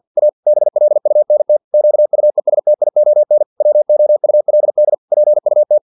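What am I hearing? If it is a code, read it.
Morse code, heard as a keyed tone, about 50 wpm, 600 Hz. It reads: A BLUNT QUESTION WOULD PUT
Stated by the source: Morse practice text from an English-language novel at 49 wpm.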